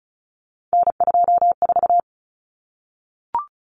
Morse code, heard as a keyed tone, about 34 wpm, 700 Hz. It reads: N24
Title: Common Cut Numbers - 3-Digit 35wpm